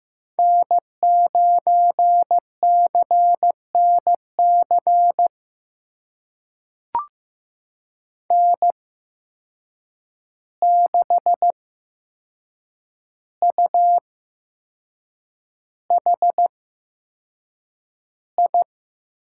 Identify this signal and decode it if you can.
Morse code, heard as a keyed tone, about 15 wpm, spaced as for 6 wpm, 700 Hz. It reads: N9CNC N6UHI
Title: USA Call Signs - 5 Characters - 8x Character Spacing - 15wpm